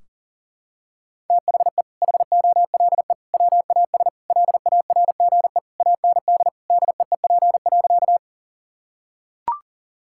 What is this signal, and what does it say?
Morse code, heard as a keyed tone, about 37 wpm, 700 Hz. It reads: THE HOLE WAS LARGE AND DEEP.
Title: Sentences from Top 1600 Words 40wpm